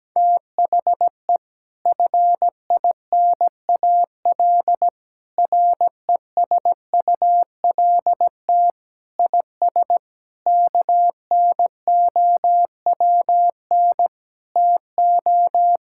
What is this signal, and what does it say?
Morse code, heard as a keyed tone, about 17 wpm, 700 Hz. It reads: THE FINAL RESULT IS KNOWN TO
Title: Sentences from Top 1000 Words - Review Part 5 - 17wpm